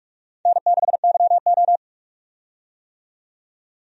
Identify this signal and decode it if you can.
Morse code, heard as a keyed tone, about 44 wpm, 700 Hz. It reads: N6YO
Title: USA Call Signs - 4 Characters - 45wpm